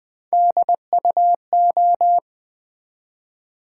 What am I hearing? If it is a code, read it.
Morse code, heard as a keyed tone, about 20 wpm, 700 Hz. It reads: DUO
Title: Build A Word - 1 Letter At A Time - 3-Letter Words - 20wpm